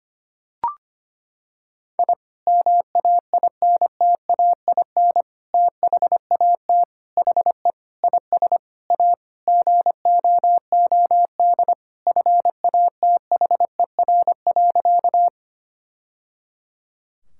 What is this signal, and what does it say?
Morse code, heard as a keyed tone, about 25 wpm, 700 Hz. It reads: I MAINTAIN THAT HE IS A GOOD FATHER.
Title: Sentences from Top 1000 Words - Review Part 5 - 25wpm